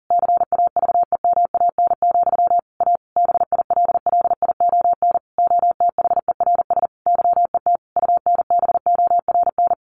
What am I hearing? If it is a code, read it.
Morse code, heard as a keyed tone, about 40 wpm, 700 Hz. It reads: CAVEMAN, A BILLION OTHERS YET UNBORN
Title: A Sound Of Thunder 40wpm